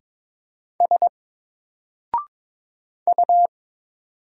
Morse code, heard as a keyed tone, about 22 wpm, 700 Hz. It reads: S U
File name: Single-Letter-Number-22wpm